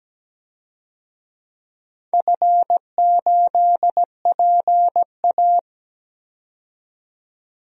Morse code, heard as a keyed tone, about 17 wpm, 700 Hz. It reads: F8PA
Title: DXCC-100 Call Signs - 4 Characters - 17wpm